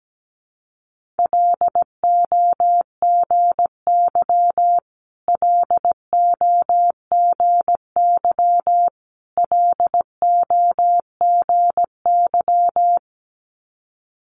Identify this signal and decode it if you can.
Morse code, heard as a keyed tone, about 17 wpm, 700 Hz. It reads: LOGY LOGY LOGY
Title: Multi-letter Phonemes Repeated 3x 17wpm